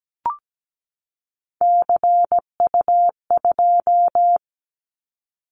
Morse code, heard as a keyed tone, about 17 wpm, 700 Hz. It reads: CU2
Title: Callsign Prefixes for Top 100 DXCC - 17wpm